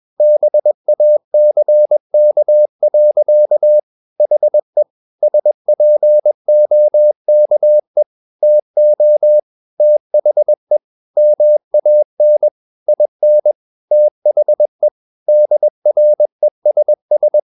Morse code, heard as a keyed tone, about 21 wpm, 600 Hz. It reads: BACK. HE SPOKE TO THE MAN IN THE DRESS
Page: CW med Gnister